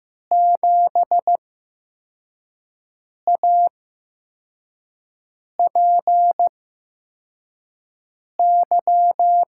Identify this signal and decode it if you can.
Morse code, heard as a keyed tone, about 15 wpm, spaced as for 4 wpm, 700 Hz. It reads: 7APY
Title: USA Call Signs - 6 Characters - 8x Character Spacing - 15wpm